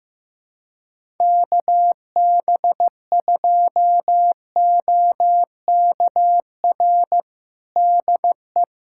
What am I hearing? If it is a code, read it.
Morse code, heard as a keyed tone, about 15 wpm, 700 Hz. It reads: KB2OKR DE
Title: USA Callsign DE Callsign - 6 Characters - 15wpm